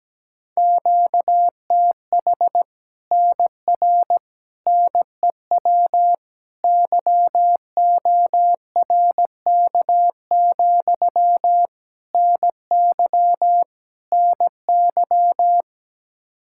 Morse code, heard as a keyed tone, about 17 wpm, 700 Hz. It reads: QTH NR NEW YORK, NY NY
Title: Practice QSO - Exchange 1 - Beginning - 17wpm